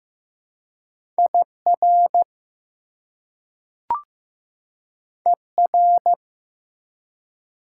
Morse code, heard as a keyed tone, about 15 wpm, 700 Hz. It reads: IR ER
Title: Prefixes and Suffixes 15wpm